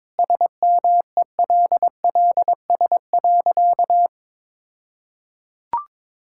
Morse code, heard as a keyed tone, about 22 wpm, 700 Hz. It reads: SMELLS.
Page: morse-code-ninja-practice-22wpm